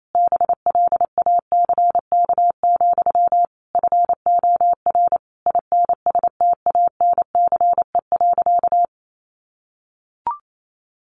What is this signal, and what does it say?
Morse code, heard as a keyed tone, about 28 wpm, 700 Hz. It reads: BLACK, FOR INSTANCE.
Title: Sentences from Top 800 Words 28wpm